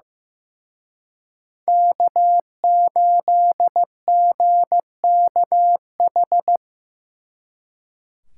Morse code, heard as a keyed tone, about 15 wpm, 700 Hz. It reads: K8GKH
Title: Speed Racing - USA 1x3 Call Signs 15wpm